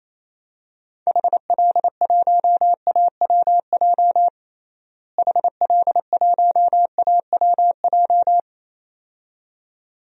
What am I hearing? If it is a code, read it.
Morse code, heard as a keyed tone, about 28 wpm, 700 Hz. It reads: HL1AWJ HL1AWJ
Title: DXCC-100 Call Signs - 6 Characters - Repeated 2x - 28wpm